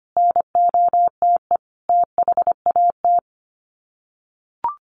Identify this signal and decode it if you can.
Morse code, heard as a keyed tone, about 25 wpm, 700 Hz. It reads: NOTE THAT
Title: Sets of 5 Words Encore2 25wpm